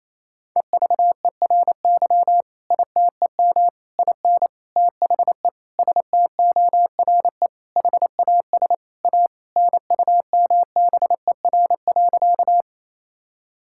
Morse code, heard as a keyed tone, about 27 wpm, 700 Hz. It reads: EVERY ITEM IN THE STORE HAS A NUMBER.
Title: Sentences from Top 1000 Words - Review Part 6 - 28wpm